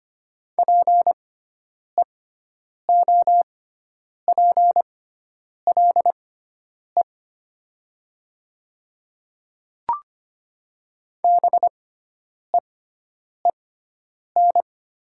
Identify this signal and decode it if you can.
Morse code, heard as a keyed tone, about 25 wpm, spaced as for 9 wpm, 700 Hz. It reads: PEOPLE BEEN
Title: 6x Character Spacing - Top 100 Words - 25wpm